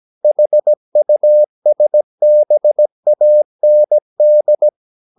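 Morse code, heard as a keyed tone, about 17 wpm, 600 Hz. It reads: HUSBAND
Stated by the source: 17 wpm Morse practice sending the English text of a novel.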